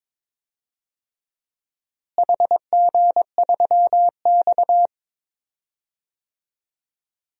Morse code, heard as a keyed tone, about 23 wpm, 700 Hz. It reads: HG3X
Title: DXCC-100 Call Signs - 4 Characters - Repeated 2x - 22wpm